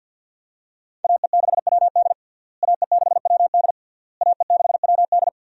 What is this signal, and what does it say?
Morse code, heard as a keyed tone, about 46 wpm, 700 Hz. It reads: AE6WD AE6WD AE6WD
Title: USA Call Signs - 5 Characters - Repeated 3x - 50wpm